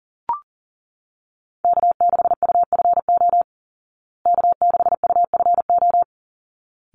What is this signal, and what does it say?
Morse code, heard as a keyed tone, about 37 wpm, 700 Hz. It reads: K6UFO K6UFO
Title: Call Signs Repeated 3x - 40wpm